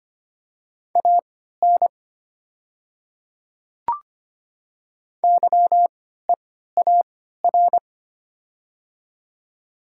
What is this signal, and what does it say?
Morse code, heard as a keyed tone, about 25 wpm, spaced as for 14 wpm, 700 Hz. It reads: AN YEAR
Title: 3x Character Spacing - Top 200 Words - 25wpm